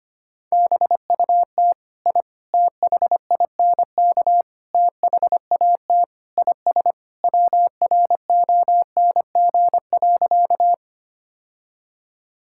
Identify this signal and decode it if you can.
Morse code, heard as a keyed tone, about 25 wpm, 700 Hz. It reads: BUT I THINK THAT IS WRONG.
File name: practice / 25wpm / Sentences with 4-Word Phrases - 25wpm